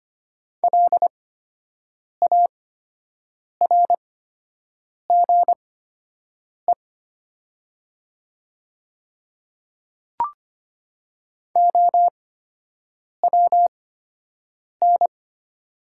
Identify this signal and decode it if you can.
Morse code, heard as a keyed tone, about 25 wpm, spaced as for 7 wpm, 700 Hz. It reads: LARGE OWN
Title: 8x Character Spacing - Top 300 Words - 25wpm